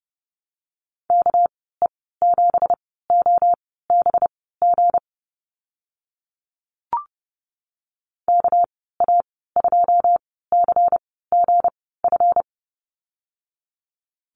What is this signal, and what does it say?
Morse code, heard as a keyed tone, about 30 wpm, spaced as for 17 wpm, 700 Hz. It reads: KE7OBG KA2CGF
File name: Lesson 37g - Letter x - Call Signs 30wpm